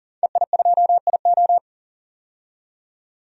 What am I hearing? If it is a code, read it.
Morse code, heard as a keyed tone, about 40 wpm, 700 Hz. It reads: EI2IO
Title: DXCC-100 Call Signs - 5 Characters - 40wpm